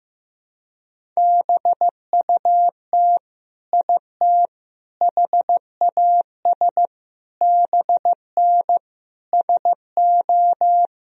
Morse code, heard as a keyed tone, about 15 wpm, 700 Hz. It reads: BUT IT HAS BN SO
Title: Sentences from Top 100 Words and Abbreviations 15wpm